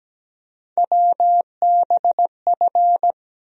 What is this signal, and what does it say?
Morse code, heard as a keyed tone, about 17 wpm, 700 Hz. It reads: WBF
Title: USA Call Sign Postfixes - 3 Letters 17wpm